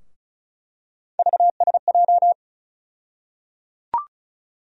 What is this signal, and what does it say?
Morse code, heard as a keyed tone, about 36 wpm, 700 Hz. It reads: VSJ
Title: USA Call Sign Postfixes - 3 Letters 35wpm